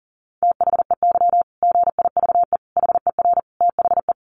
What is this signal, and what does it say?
Morse code, heard as a keyed tone, about 37 wpm, 700 Hz. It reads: THEY GIVE HER THE
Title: Easy Phrases 40wpm